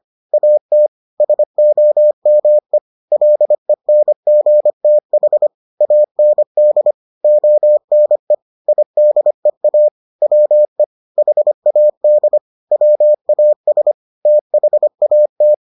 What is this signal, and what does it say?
Morse code, heard as a keyed tone, about 25 wpm, 600 Hz. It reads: AT SOME LENGTH AND ONE IDEA WE HAD WAS THAT